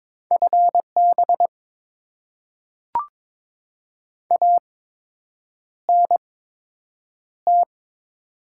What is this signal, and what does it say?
Morse code, heard as a keyed tone, about 22 wpm, spaced as for 8 wpm, 700 Hz. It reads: FB ANT
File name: Top 20 QSO Elements - 8x Character Spacing - 22wpm